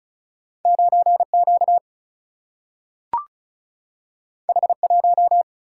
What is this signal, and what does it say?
Morse code, heard as a keyed tone, about 36 wpm, 700 Hz. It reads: 9 Q H 1
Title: Single-Letter Single-Number 35wpm